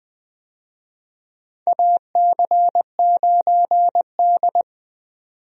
Morse code, heard as a keyed tone, about 20 wpm, 700 Hz. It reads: AC9D
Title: USA Call Signs - 4 Characters - 20wpm